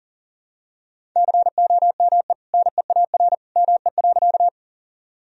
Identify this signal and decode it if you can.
Morse code, heard as a keyed tone, about 40 wpm, 700 Hz. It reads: COME NEAR ME.
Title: Sentences from Top 1000 Words - Review Part 1 - 40wpm